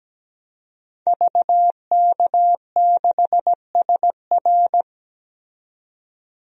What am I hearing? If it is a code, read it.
Morse code, heard as a keyed tone, about 17 wpm, 700 Hz. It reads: VK6SR
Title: DXCC-100 Call Signs - 5 Characters - 17wpm